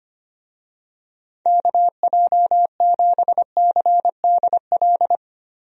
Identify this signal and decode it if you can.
Morse code, heard as a keyed tone, about 25 wpm, 700 Hz. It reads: KJ7CDL
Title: USA Call Signs - 6 Characters - Repeated 3x - 25wpm